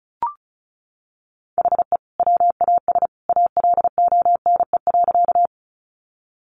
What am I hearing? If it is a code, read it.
Morse code, heard as a keyed tone, about 34 wpm, 700 Hz. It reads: HE WAS ALONE.
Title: Sentences from Top 1000 Words 35wpm